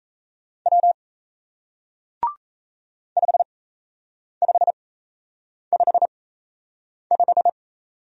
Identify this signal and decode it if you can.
Morse code, heard as a keyed tone, about 38 wpm, spaced as for 8 wpm, 700 Hz. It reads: W 5555